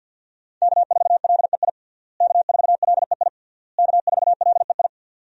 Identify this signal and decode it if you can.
Morse code, heard as a keyed tone, about 48 wpm, 700 Hz. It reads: K4LEI K4LEI K4LEI
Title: USA Call Signs - 5 Characters - Repeated 3x - 50wpm